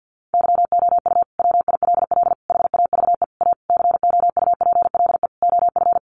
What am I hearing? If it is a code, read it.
Morse code, heard as a keyed tone, about 50 wpm, 700 Hz. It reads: YOU WILL HAVE A COUPLE OF
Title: Sentences from Top 1100 Words 50wpm